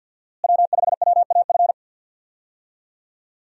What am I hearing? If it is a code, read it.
Morse code, heard as a keyed tone, about 52 wpm, 700 Hz. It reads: W5WAF
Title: USA Call Signs - 5 Characters - Repeated 3x - 50wpm